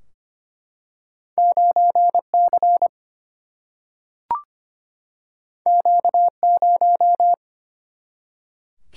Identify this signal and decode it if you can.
Morse code, heard as a keyed tone, about 25 wpm, 700 Hz. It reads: 9 C Q 0
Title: Single-Letter Single-Number 25wpm